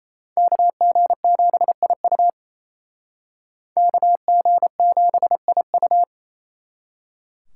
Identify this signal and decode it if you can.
Morse code, heard as a keyed tone, about 35 wpm, 700 Hz. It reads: KG7IU KG7IU
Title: Speed Racing - USA 2x2 Call Signs 28wpm